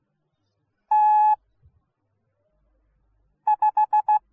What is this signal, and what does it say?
Morse code, heard as a keyed tone, about 16 wpm, 850 Hz. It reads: T 5